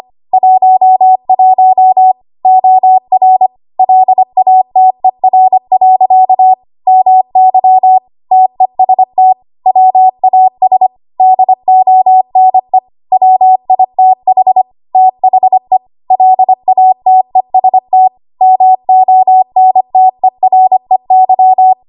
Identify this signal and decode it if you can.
Morse code, heard as a keyed tone, about 25 wpm, 750 Hz. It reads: 11 OR LATER. MY TEST WAS DONE WITH THE LATEST MONTEREY
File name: Archive / 25 WPM / 25WPM